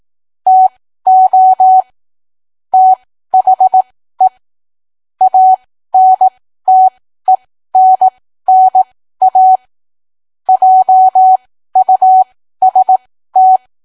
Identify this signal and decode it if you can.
Morse code, heard as a keyed tone, about 18 wpm, spaced as for 13 wpm, 750 Hz. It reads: TO THE ANTENNA JUST